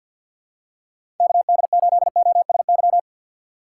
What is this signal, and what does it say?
Morse code, heard as a keyed tone, about 53 wpm, 700 Hz. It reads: KD8OSY
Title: USA Call Signs - 6 Characters - 4x Character Spacing - 50wpm